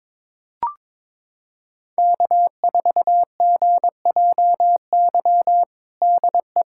Morse code, heard as a keyed tone, about 22 wpm, 700 Hz. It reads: K4GJY DE
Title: USA Callsign DE Callsign - 5 Characters - 22wpm